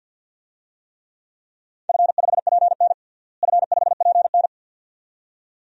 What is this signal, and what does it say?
Morse code, heard as a keyed tone, about 48 wpm, 700 Hz. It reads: F5PN F5PN
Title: DXCC-100 Call Signs - 4 Characters - Repeated 2x - 50wpm